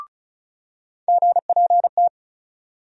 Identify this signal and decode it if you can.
Morse code, heard as a keyed tone, about 36 wpm, 700 Hz. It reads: GPT
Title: USA Call Sign Postfixes - 3 Letters 35wpm